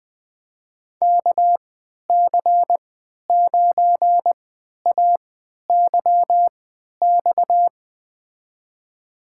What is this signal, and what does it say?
Morse code, heard as a keyed tone, about 20 wpm, spaced as for 11 wpm, 700 Hz. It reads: KC9AYX
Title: USA Call Signs - 6 Characters - 3x Character Spacing - 20wpm